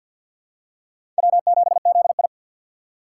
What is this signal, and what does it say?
Morse code, heard as a keyed tone, about 53 wpm, 700 Hz. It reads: W7ZI
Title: USA Call Signs - 4 Characters - 6x Character Spacing - 50wpm